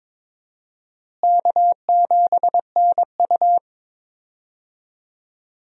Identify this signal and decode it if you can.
Morse code, heard as a keyed tone, about 22 wpm, 700 Hz. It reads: K7NU
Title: USA Call Signs - 4 Characters - Repeated 3x - 22wpm